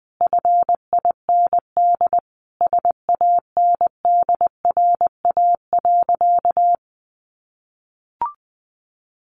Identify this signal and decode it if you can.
Morse code, heard as a keyed tone, about 20 wpm, 700 Hz. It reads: FIND SANDRA.
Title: Easy Phrases 20wpm